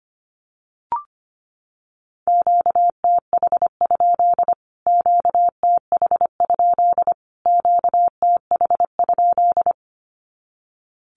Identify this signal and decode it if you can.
Morse code, heard as a keyed tone, about 25 wpm, 700 Hz. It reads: QTH? QTH? QTH?
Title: Q-Codes Repeated 3x 25wpm